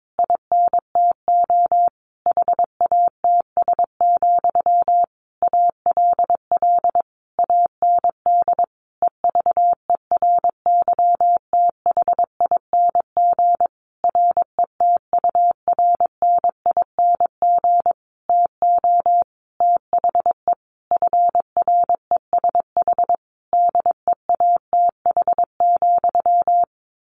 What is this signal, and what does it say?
Morse code, heard as a keyed tone, about 22 wpm, 700 Hz. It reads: INTO HATS, ALL AND EVERYTHING RETURNING TO THE FRESH DEATH,
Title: A Sound Of Thunder 22wpm